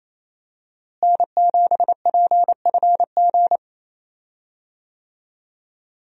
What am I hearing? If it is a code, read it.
Morse code, heard as a keyed tone, about 28 wpm, 700 Hz. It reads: N7PFG
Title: USA Call Signs - 5 Characters - Repeated 2x - 28wpm